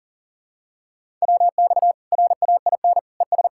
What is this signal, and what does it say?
Morse code, heard as a keyed tone, about 37 wpm, 700 Hz. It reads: WX RAIN ES
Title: Practice QSO - Exchange 2 - Info - 40wpm